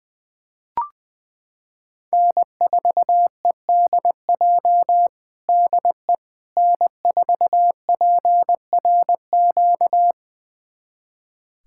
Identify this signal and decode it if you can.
Morse code, heard as a keyed tone, about 20 wpm, 700 Hz. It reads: N4EDJ DE N4PRQ
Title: USA Callsign DE Callsign - 5 Characters - 20wpm